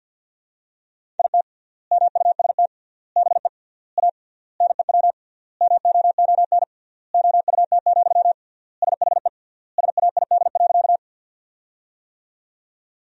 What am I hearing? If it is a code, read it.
Morse code, heard as a keyed tone, about 51 wpm, 700 Hz. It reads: IT MUST BE A NEW MOON OUT, SHE SAID.
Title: 3x Word-Spacing - Sentences from Top 500 Words - 50wpm